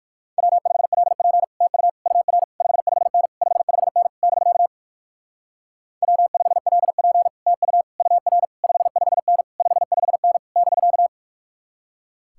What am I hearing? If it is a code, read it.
Morse code, heard as a keyed tone, about 56 wpm, 700 Hz. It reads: W5LP TU UR 55N 55N <BK> W5LP TU UR 55N 55N <BK>
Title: Speed-Racing Common QSO Phrases - 45wpm